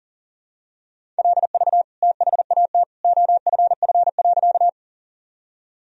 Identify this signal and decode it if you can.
Morse code, heard as a keyed tone, about 40 wpm, 700 Hz. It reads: LV THAT OFF.
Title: Sentences from Top 400 Words and Abbreviations 40wpm